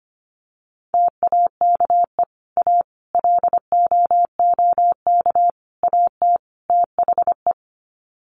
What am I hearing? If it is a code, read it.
Morse code, heard as a keyed tone, about 25 wpm, 700 Hz. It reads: TAKE A LOOK AT THE
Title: Sets of 5 Words 25wpm